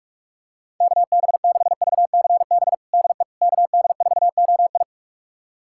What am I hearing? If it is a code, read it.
Morse code, heard as a keyed tone, about 46 wpm, 700 Hz. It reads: KB6VCB DE KD4OI
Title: USA Callsign DE Callsign 45wpm